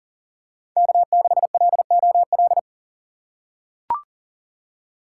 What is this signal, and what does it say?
Morse code, heard as a keyed tone, about 40 wpm, 700 Hz. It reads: K6LOL
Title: USA Call Signs - 5 Characters - 6x Character Spacing - 40wpm